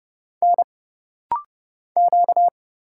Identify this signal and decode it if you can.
Morse code, heard as a keyed tone, about 30 wpm, 700 Hz. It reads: N Q
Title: Single Letters - Rapid-Fire 30wpm